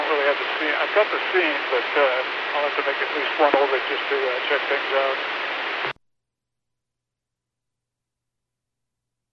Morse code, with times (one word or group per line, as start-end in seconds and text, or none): none